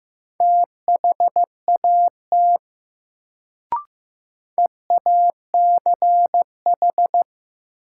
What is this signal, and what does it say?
Morse code, heard as a keyed tone, about 16 wpm, 700 Hz. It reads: THAT EACH
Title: Sets of 2 Words - New Word Drill - 15wpm